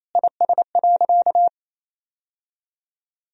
Morse code, heard as a keyed tone, about 28 wpm, 700 Hz. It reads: IS.
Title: Sentences from Top 1000 Words - Review Part 4 - 28wpm